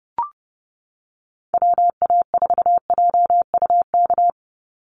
Morse code, heard as a keyed tone, about 30 wpm, 700 Hz. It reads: WA4JUK
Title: Call Signs Repeated 3x - 30wpm